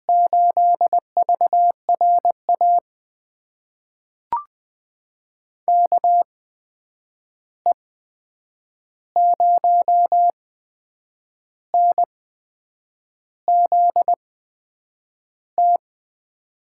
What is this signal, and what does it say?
Morse code, heard as a keyed tone, about 20 wpm, spaced as for 8 wpm, 700 Hz. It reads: 8VRA KE0NZT
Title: USA Call Signs - 6 Characters - 8x Character Spacing - 20wpm